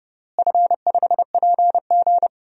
Morse code, heard as a keyed tone, about 30 wpm, 700 Hz. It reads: F5PG
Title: DXCC-100 Call Signs - 4 Characters - Repeated 2x - 30wpm